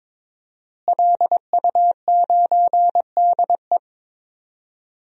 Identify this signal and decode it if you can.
Morse code, heard as a keyed tone, about 22 wpm, 700 Hz. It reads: LU9DE
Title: DXCC-100 Call Signs - 5 Characters - Repeated 3x - 22wpm